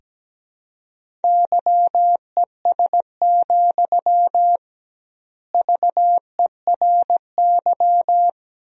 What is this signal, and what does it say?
Morse code, heard as a keyed tone, about 17 wpm, 700 Hz. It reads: YES, VERY
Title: 2x Word-Spacing - Sentences from Top 500 Words - 17wpm